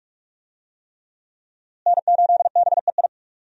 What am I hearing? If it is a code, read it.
Morse code, heard as a keyed tone, about 43 wpm, 700 Hz. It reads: N8BEI
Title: USA Call Signs - 5 Characters - Repeated 3x - 45wpm